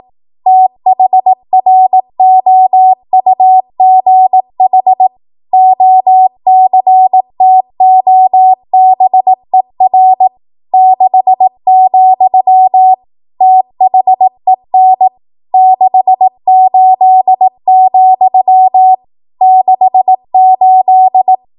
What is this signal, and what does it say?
Morse code, heard as a keyed tone, about 18 wpm, 750 Hz. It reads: THROUGH OCTOBER 6, THEN 68, 68
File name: Archive / 18 WPM / 18WPM